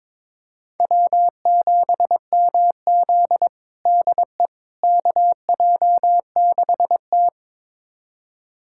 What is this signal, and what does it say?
Morse code, heard as a keyed tone, about 22 wpm, 700 Hz. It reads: W7MZ DE KJ6T
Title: USA Callsign DE Callsign - 4 Characters - 22wpm